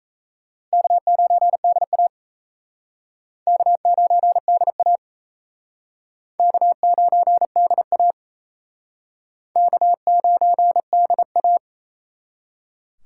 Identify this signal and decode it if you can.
Morse code, heard as a keyed tone, about 34 wpm, 700 Hz. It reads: K9DA K9DA K9DA K9DA
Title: Speed Racing - USA 1x2 Call Signs 28wpm